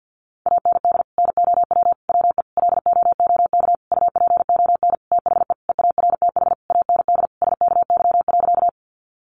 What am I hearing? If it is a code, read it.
Morse code, heard as a keyed tone, about 50 wpm, 700 Hz. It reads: AND NOW WE LOOK UPON THE EARTH AND SKY.
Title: Sentences from Top 1200 Words 50wpm